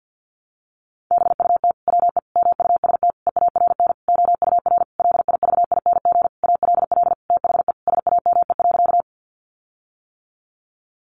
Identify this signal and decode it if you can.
Morse code, heard as a keyed tone, about 50 wpm, 700 Hz. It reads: BUT WE MUST EARN OUR LIVING ALL THE SAME.
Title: Sentences from Top 1500 Words 50wpm